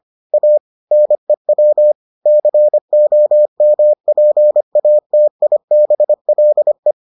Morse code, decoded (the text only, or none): A NEW COMPATIBLE